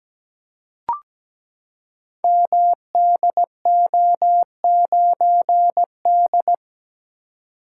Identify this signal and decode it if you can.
Morse code, heard as a keyed tone, about 17 wpm, 700 Hz. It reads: MDO9D
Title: Code Groups - 5 Letter:Numbers - 17wpm